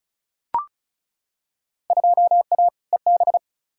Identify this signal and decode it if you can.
Morse code, heard as a keyed tone, about 36 wpm, 700 Hz. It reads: 2A EB
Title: Field Day Exchanges - 35wpm